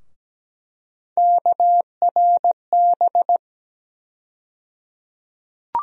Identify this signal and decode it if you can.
Morse code, heard as a keyed tone, about 17 wpm, 700 Hz. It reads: KRB
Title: USA Call Sign Postfixes - 3 Letters 17wpm